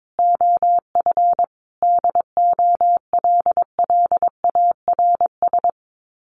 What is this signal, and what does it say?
Morse code, heard as a keyed tone, about 22 wpm, 700 Hz. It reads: OF DOLLARS
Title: Sets of 5 Words 22wpm